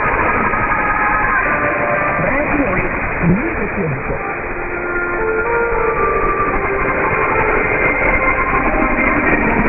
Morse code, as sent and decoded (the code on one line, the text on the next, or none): none